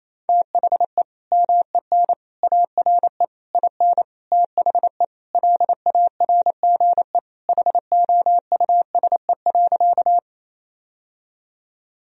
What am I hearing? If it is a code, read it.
Morse code, heard as a keyed tone, about 28 wpm, 700 Hz. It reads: THE MEN ARE IN THE LARGE HOUSE.
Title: Sentences from Top 1000 Words - Review Part 1 - 28wpm